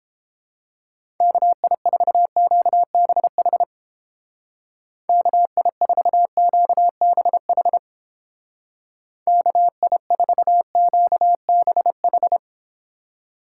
Morse code, heard as a keyed tone, about 29 wpm, 700 Hz. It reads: KI4QBH KI4QBH KI4QBH